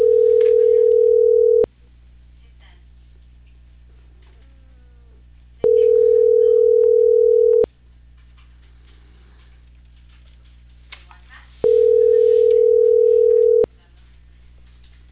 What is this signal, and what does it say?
Ringback tone (North America), 440 + 480 Hz.